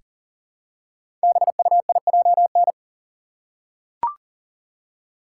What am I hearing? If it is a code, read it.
Morse code, heard as a keyed tone, about 38 wpm, 700 Hz. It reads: BUIJN
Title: Code Groups - 5 Letters - 40wpm